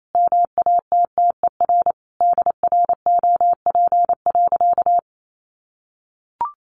Morse code, heard as a keyed tone, about 28 wpm, 700 Hz. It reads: MATTER DROP.
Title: Sentences from Top 1400 Words 28wpm